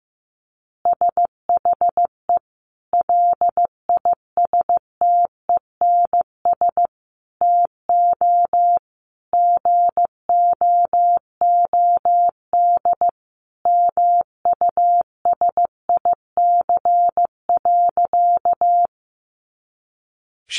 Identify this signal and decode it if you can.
Morse code, heard as a keyed tone, about 15 wpm, 700 Hz. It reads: SHE LISTENS TO GOOD MUSIC.
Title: Easy Phrases 15wpm